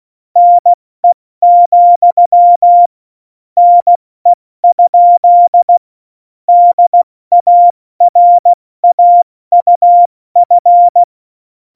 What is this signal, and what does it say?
Morse code, heard as a keyed tone, about 16 wpm, spaced as for 14 wpm, 700 Hz. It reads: NE, NE? DARAUF